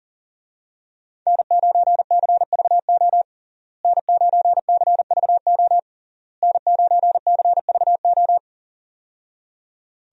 Morse code, heard as a keyed tone, about 40 wpm, 700 Hz. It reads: N9CVO N9CVO N9CVO
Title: USA Call Signs - 5 Characters - Repeated 3x - 40wpm